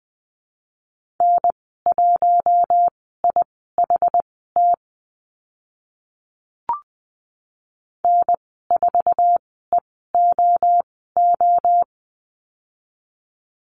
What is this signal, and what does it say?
Morse code, heard as a keyed tone, about 20 wpm, spaced as for 14 wpm, 700 Hz. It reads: N1IHT N4EOO
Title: 122 Lesson 20g - Number 3 - Call Signs 20wpm